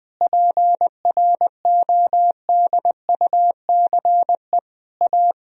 Morse code, heard as a keyed tone, about 20 wpm, 700 Hz. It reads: PRODUCE A
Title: Sentences from Top 1000 Words - Review Part 3 - 20wpm